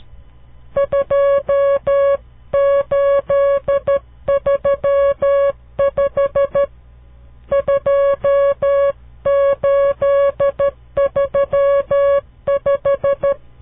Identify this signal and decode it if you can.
Morse code, heard as a keyed tone, about 12 wpm, 560 Hz. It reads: 2835 2835
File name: CW Learning Programs / farnsworth / unit 5